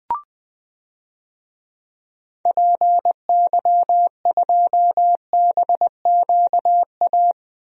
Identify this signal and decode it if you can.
Morse code, heard as a keyed tone, about 21 wpm, 700 Hz. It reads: PY2BQA
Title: DXCC-100 Call Signs - 6 Characters - Repeated 3x - 20wpm